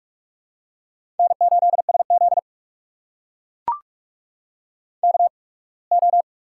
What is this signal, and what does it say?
Morse code, heard as a keyed tone, about 42 wpm, spaced as for 26 wpm, 700 Hz. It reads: N8SZ KO
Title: USA Call Signs - 4 Characters - 8x Character Spacing - 45wpm